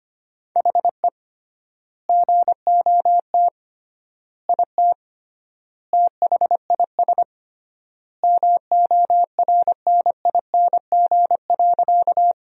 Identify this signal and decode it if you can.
Morse code, heard as a keyed tone, about 25 wpm, 700 Hz. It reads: HE GOT IT THIS MORNING.
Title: 3x Word-Spacing - Sentences from Top 400 Words - 25wpm